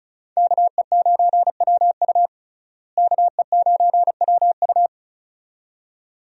Morse code, heard as a keyed tone, about 36 wpm, 700 Hz. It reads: KE9WU KE9WU
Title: USA Call Signs - 5 Characters - Repeated 2x - 35wpm